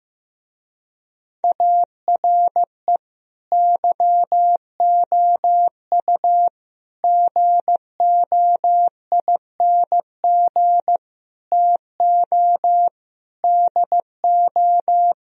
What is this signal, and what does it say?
Morse code, heard as a keyed tone, about 16 wpm, 700 Hz. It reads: ARE YOU GOING TO DO
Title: Speed-Racing - Sets of 5 Words 15wpm